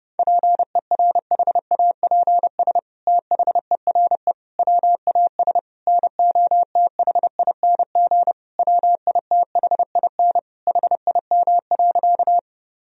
Morse code, heard as a keyed tone, about 30 wpm, 700 Hz. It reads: PERHAPS THERE WAS NOTHING WITHIN HIM.
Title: Sentences from Top 1000 Words - Review Part 4 - 30wpm